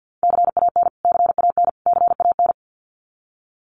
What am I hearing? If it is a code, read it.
Morse code, heard as a keyed tone, about 50 wpm, 700 Hz. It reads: CAN CAN CAN
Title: Top 500 Words Repeated 50wpm